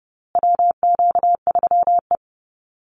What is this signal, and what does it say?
Morse code, heard as a keyed tone, about 30 wpm, 700 Hz. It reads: WQ3E
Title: Call Signs Repeated 2x - 30wpm